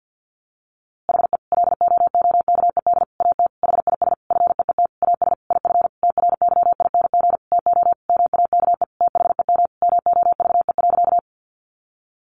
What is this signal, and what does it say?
Morse code, heard as a keyed tone, about 50 wpm, 700 Hz. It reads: HE LOOKED AT HIS FEET AS IF TRYING TO MAKE THEM MOVE.
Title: A Sound Of Thunder 50wpm